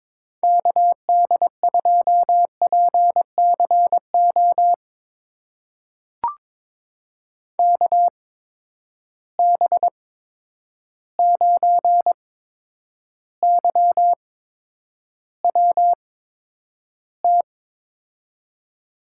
Morse code, heard as a keyed tone, about 23 wpm, spaced as for 10 wpm, 700 Hz. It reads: KD2PCO KB9YWT
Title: USA Call Signs - 6 Characters - 8x Character Spacing - 22wpm